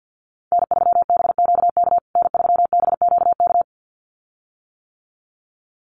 Morse code, heard as a keyed tone, about 50 wpm, 700 Hz. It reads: N3BQK N3BQK
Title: USA Call Signs Repeated 2x 50wpm